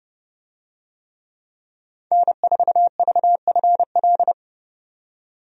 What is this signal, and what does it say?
Morse code, heard as a keyed tone, about 30 wpm, 700 Hz. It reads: N4VFL
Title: USA Call Signs - 5 Characters - 30wpm